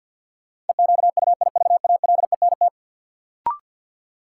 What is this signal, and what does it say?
Morse code, heard as a keyed tone, about 47 wpm, 700 Hz. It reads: EQUIVALENT
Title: Sentences from Top 1600 Words - New Word Drill - 50wpm